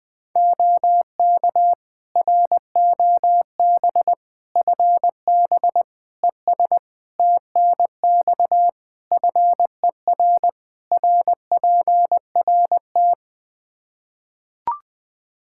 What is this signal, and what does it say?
Morse code, heard as a keyed tone, about 20 wpm, 700 Hz. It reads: OK ROB FB ES TNX FER RPRT
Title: Practice QSO - Exchange 2 - Info - 20wpm